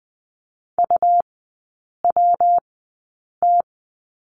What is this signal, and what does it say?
Morse code, heard as a keyed tone, about 20 wpm, spaced as for 8 wpm, 700 Hz. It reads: UWT
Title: 198 Lesson 32e - Letter k - Three Characters 20wpm